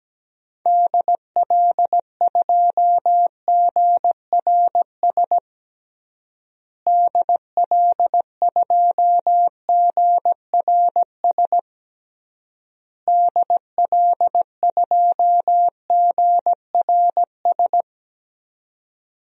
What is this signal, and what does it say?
Morse code, heard as a keyed tone, about 17 wpm, 700 Hz. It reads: DL2GRS DL2GRS DL2GRS
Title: DXCC-100 Call Signs - 6 Characters - Repeated 3x - 17wpm